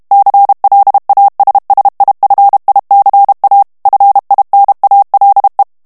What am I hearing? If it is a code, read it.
Morse code, heard as a keyed tone, about 32 wpm, 800 Hz. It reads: CLASSIFICA FINALE
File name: ITALIEN texte clair 2 vitesse 30 mots minute